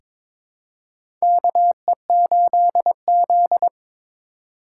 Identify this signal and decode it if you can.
Morse code, heard as a keyed tone, about 21 wpm, 700 Hz. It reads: KE8Z
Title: USA Call Signs - 4x Character Spacing - 22wpm